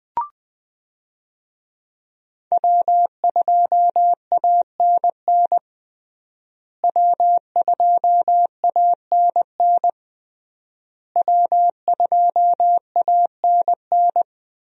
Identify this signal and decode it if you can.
Morse code, heard as a keyed tone, about 20 wpm, 700 Hz. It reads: W2ANN W2ANN W2ANN
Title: USA Call Signs - 5 Characters - Repeated 3x - 20wpm